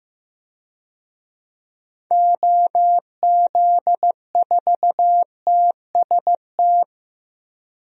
Morse code, heard as a keyed tone, about 15 wpm, 700 Hz. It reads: OZ4TST
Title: DXCC-100 Call Signs - 6 Characters - Repeated 3x - 15wpm